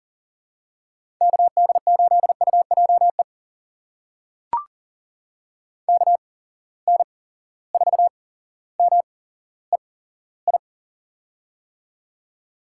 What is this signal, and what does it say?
Morse code, heard as a keyed tone, about 40 wpm, spaced as for 17 wpm, 700 Hz. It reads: KD8UJE KN4MEI
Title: USA Call Signs - 6 Characters - 8x Character Spacing - 40wpm